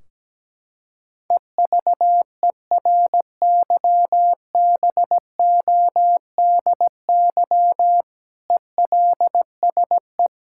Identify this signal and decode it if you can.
Morse code, heard as a keyed tone, about 17 wpm, 700 Hz. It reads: EVERYBODY ELSE
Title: Sentences from Top 1600 Words 17wpm